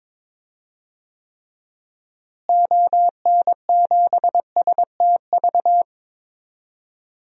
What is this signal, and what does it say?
Morse code, heard as a keyed tone, about 22 wpm, 700 Hz. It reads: ON7STV
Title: DXCC-100 Call Signs - 6 Characters - Repeated 2x - 22wpm